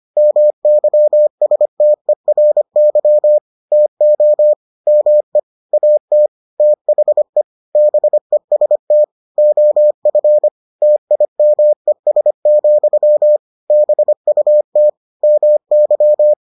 Morse code, heard as a keyed tone, about 25 wpm, 600 Hz. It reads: MYSTERY TO ME AT THE BEST OF TIMES, BUT MY